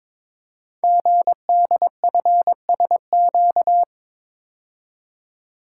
Morse code, heard as a keyed tone, about 22 wpm, 700 Hz. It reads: GDFSQ
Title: Code Groups - 5 Letters - 22wpm